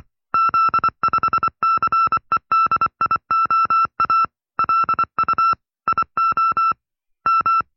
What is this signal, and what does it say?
Morse code, heard as a keyed tone, about 24 wpm, 1340 Hz. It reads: Z5CEDIOA LU IO M
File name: FreeFiles / CW-practice / 20WPM-side2